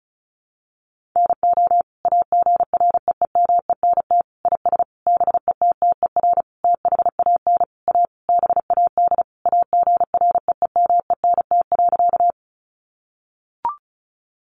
Morse code, heard as a keyed tone, about 34 wpm, 700 Hz. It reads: NO AGREEMENT IS BETTER THAN A BAD AGREEMENT.
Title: Sentences from Top 1500 Words 35wpm